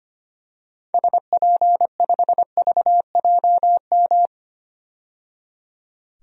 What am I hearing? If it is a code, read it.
Morse code, heard as a keyed tone, about 25 wpm, 700 Hz. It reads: SP5VJM
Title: DXCC-100 Call Signs - 6 Characters - 25wpm